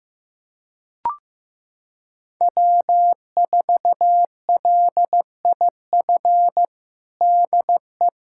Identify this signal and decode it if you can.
Morse code, heard as a keyed tone, about 15 wpm, 700 Hz. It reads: W4LIF DE
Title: USA Callsign DE Callsign - 5 Characters - 15wpm